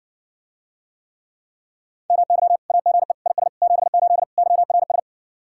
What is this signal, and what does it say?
Morse code, heard as a keyed tone, about 69 wpm, 700 Hz. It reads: MY AGE IS 78 YRS